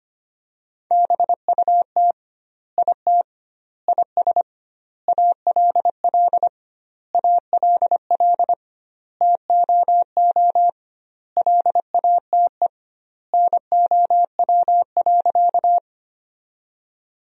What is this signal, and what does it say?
Morse code, heard as a keyed tone, about 25 wpm, 700 Hz. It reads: BUT IT IS ALL ALL TOO LATE NOW.
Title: 2x Word-Spacing - Sentences from Top 300 Words - 25wpm